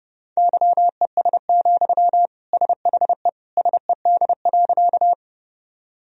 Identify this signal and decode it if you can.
Morse code, heard as a keyed tone, about 32 wpm, 700 Hz. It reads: YES, SHE SED.
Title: Sentences from Top 500 Words and Abbreviations 30wpm